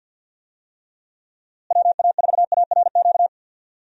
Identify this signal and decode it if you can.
Morse code, heard as a keyed tone, about 46 wpm, 700 Hz. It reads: WA4ARQ
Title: USA Call Signs - 6 Characters - Repeated 3x - 50wpm